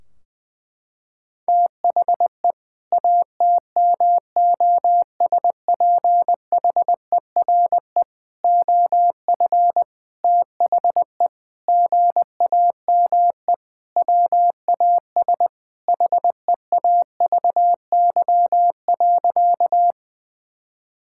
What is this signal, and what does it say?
Morse code, heard as a keyed tone, about 20 wpm, 700 Hz. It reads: THE ATMOSPHERE OF THE GAME WAS HEAVY.
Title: Sentences from Top 1600 Words 20wpm